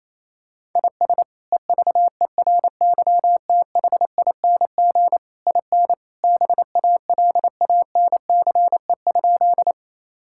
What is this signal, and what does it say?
Morse code, heard as a keyed tone, about 28 wpm, 700 Hz. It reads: IS EVERYTHING IN BALANCE?
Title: Sentences from Top 1000 Words - Review Part 6 - 28wpm